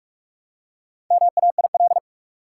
Morse code, heard as a keyed tone, about 46 wpm, 700 Hz. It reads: MAIL
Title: Four Letter Words - 45wpm